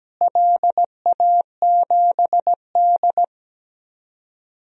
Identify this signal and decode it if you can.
Morse code, heard as a keyed tone, about 17 wpm, 700 Hz. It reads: LA7D
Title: DXCC-100 Call Signs - 4 Characters - Repeated 2x - 17wpm